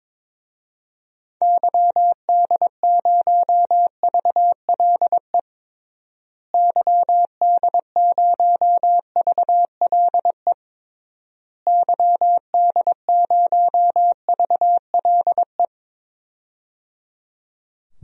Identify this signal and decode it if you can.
Morse code, heard as a keyed tone, about 22 wpm, 700 Hz. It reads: YD0VLE YD0VLE YD0VLE
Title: DXCC-100 Call Signs - 6 Characters - Repeated 3x - 22wpm